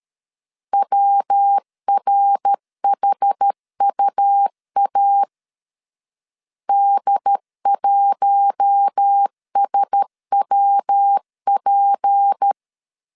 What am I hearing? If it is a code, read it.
Morse code, heard as a keyed tone, about 13 wpm, 800 Hz. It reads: WRHUA D1SWP
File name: CW / DET / 1 6 1500C S B 12wpm 3431